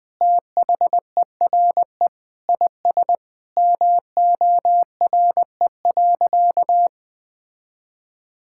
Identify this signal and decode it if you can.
Morse code, heard as a keyed tone, about 19 wpm, 700 Hz. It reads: THERE IS MORE.
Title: Sentences with 3-Word Phrases - 20wpm